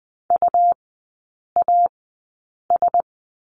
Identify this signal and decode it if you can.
Morse code, heard as a keyed tone, about 20 wpm, spaced as for 8 wpm, 700 Hz. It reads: UAS
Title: 138 Lesson 23e - Letter f - Three Characters 20wpm